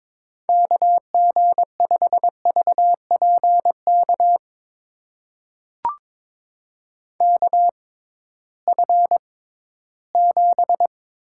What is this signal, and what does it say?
Morse code, heard as a keyed tone, about 22 wpm, spaced as for 14 wpm, 700 Hz. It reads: KG5VPK KF7
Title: USA Call Signs - 6 Characters - 6x Character Spacing - 22wpm